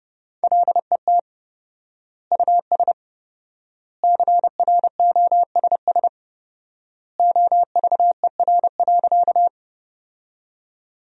Morse code, heard as a keyed tone, about 30 wpm, 700 Hz. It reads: LET US CROSS OVER.
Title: 4x Word-Spacing - Sentences from Top 300 Words - 30wpm